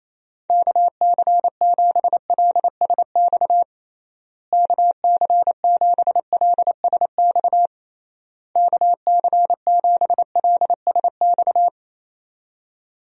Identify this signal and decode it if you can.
Morse code, heard as a keyed tone, about 28 wpm, 700 Hz. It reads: KC7LSX KC7LSX KC7LSX
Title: USA Call Signs - 6 Characters - Repeated 3x - 28wpm